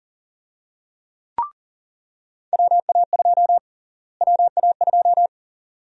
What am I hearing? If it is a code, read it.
Morse code, heard as a keyed tone, about 39 wpm, 700 Hz. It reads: WA2 WA2
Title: USA Call Sign Prefixes - Repeated 2x - 40wpm